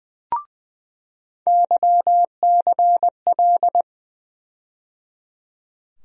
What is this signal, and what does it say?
Morse code, heard as a keyed tone, about 20 wpm, 700 Hz. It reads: YCL
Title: USA Call Sign Postfixes - 3 Letters 20wpm